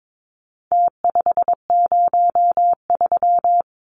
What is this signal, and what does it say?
Morse code, heard as a keyed tone, about 22 wpm, 700 Hz. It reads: T503
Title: Numbers - Contest Style - 4-Digit 22wpm